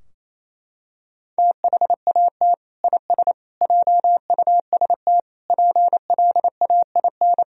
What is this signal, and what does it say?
Morse code, heard as a keyed tone, about 27 wpm, 700 Hz. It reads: THAT IS JUST PLAIN
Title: Sentences from Top 1000 Words - Review Part 4 - 28wpm